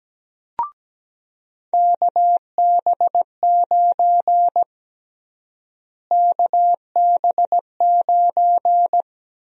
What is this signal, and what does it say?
Morse code, heard as a keyed tone, about 17 wpm, 700 Hz. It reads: KB9 KB9
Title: USA Call Sign Prefixes - Repeated 2x - 17wpm